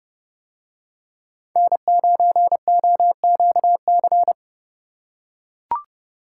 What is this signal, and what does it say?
Morse code, heard as a keyed tone, about 28 wpm, 700 Hz. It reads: N9OQC